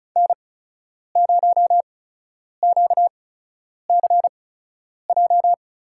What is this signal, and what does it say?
Morse code, heard as a keyed tone, about 36 wpm, spaced as for 10 wpm, 700 Hz. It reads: N0QCJ
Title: USA Call Signs - 5 Characters - 8x Character Spacing - 35wpm